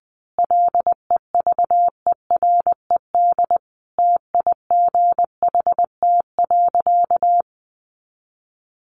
Morse code, heard as a keyed tone, about 20 wpm, 700 Hz. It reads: LEVERED TIGHT.
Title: A Sound Of Thunder 20wpm